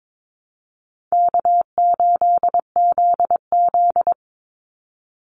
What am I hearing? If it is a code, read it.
Morse code, heard as a keyed tone, about 22 wpm, 700 Hz. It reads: K8ZZ
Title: Easy Call Signs Repeated 2x - 22wpm